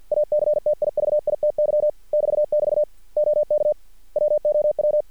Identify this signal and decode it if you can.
Morse code, heard as a keyed tone, about 50 wpm, 600 Hz. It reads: ACTIVITY <BT><BT> OK WOW
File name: archived / CRSnet